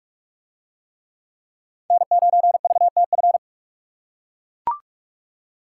Morse code, heard as a keyed tone, about 41 wpm, 700 Hz. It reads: N9VTF